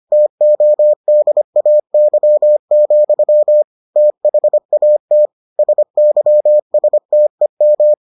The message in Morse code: TODAY, THAT SYSTEM

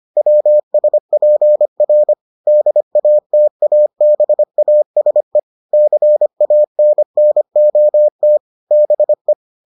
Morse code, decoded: WSPR DATABASE CANNOT BE

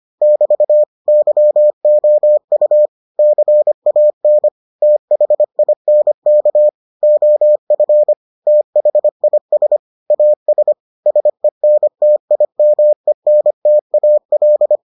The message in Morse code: <BT> YOU CAN THINK OF THIS AS SENTIMENTAL

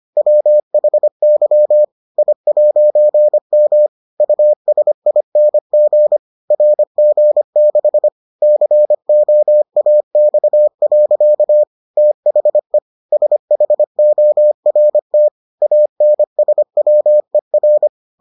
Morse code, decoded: WHY I'M USING RG6 COAX. THE SHORT ANSWER

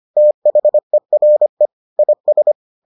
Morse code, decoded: THERE IS

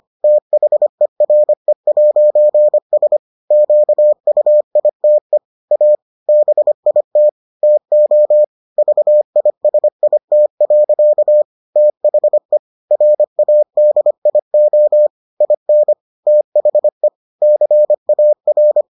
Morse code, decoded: THERE'S QUITE A BIT TO VISIT. THE RADIO IN THE CAR

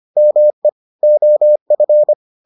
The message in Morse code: ME OF

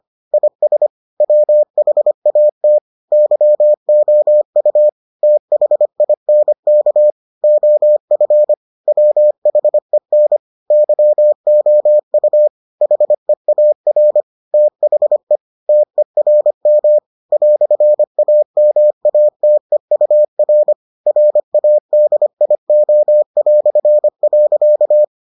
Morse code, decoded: IS WHAT YOU THINK OF WHEN YOU HEAR THE TERM "AMATEUR RADIO".

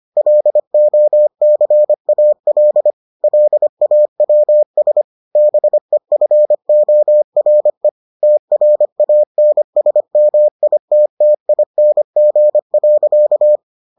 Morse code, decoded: LOCAL LAWS BEFORE TRANSMITTING.

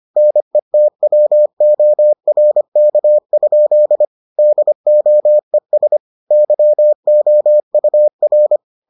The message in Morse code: NETWORK? DOES YOUR

